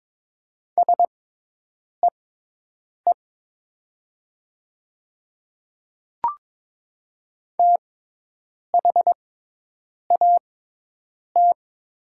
Morse code, heard as a keyed tone, about 22 wpm, spaced as for 8 wpm, 700 Hz. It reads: SEE THAT
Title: 6x Character Spacing - Top 100 Words - 22wpm